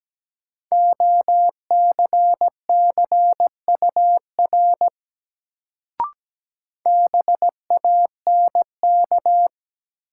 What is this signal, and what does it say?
Morse code, heard as a keyed tone, about 17 wpm, 700 Hz. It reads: OCCUR BANK
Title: Sentences from Top 900 Words - New Word Drill - 17wpm